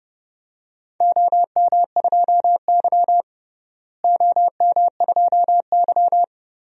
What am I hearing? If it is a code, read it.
Morse code, heard as a keyed tone, about 31 wpm, 700 Hz. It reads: OM2Y OM2Y
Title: DXCC-100 Call Signs - 4 Characters - Repeated 2x - 30wpm